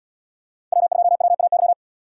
Morse code, heard as a keyed tone, about 78 wpm, 700 Hz. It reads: W2WAY